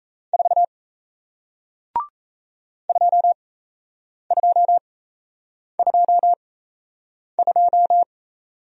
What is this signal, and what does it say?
Morse code, heard as a keyed tone, about 37 wpm, spaced as for 8 wpm, 700 Hz. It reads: 4 2222